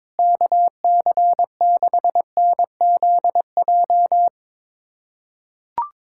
Morse code, heard as a keyed tone, about 22 wpm, 700 Hz. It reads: KC6NZJ
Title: USA Callsign DE Callsign 22wpm